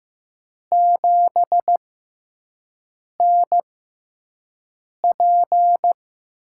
Morse code, heard as a keyed tone, about 15 wpm, spaced as for 5 wpm, 700 Hz. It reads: 7NP
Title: USA Call Signs - 4 Characters - 6x Character Spacing - 15wpm